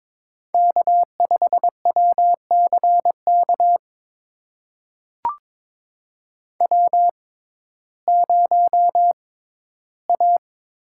Morse code, heard as a keyed tone, about 22 wpm, spaced as for 8 wpm, 700 Hz. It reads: K5WC KW0A
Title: USA Call Signs - 6x Character Spacing - 22wpm